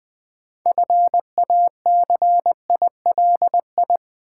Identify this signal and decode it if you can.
Morse code, heard as a keyed tone, about 20 wpm, 700 Hz. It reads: FACILI
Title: Build A Word - 2 Letters At A Time - 8-Letter Words - 20wpm